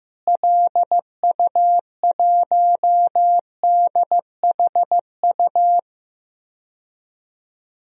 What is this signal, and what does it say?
Morse code, heard as a keyed tone, about 15 wpm, 700 Hz. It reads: LU1DHU
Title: DXCC-100 Call Signs - 6 Characters - Repeated 2x - 15wpm